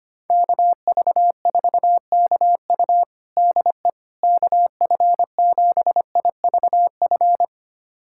Morse code, heard as a keyed tone, about 25 wpm, 700 Hz. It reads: KV4KU DE KF7IVF
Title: USA Callsign DE Callsign 25wpm